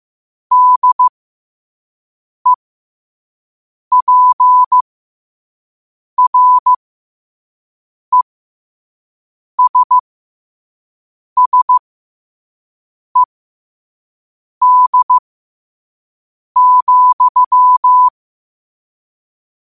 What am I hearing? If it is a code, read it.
Morse code, heard as a keyed tone, about 15 wpm, spaced as for 5 wpm, 1000 Hz. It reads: DEPRESSED,
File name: qotdfiles / 5wpm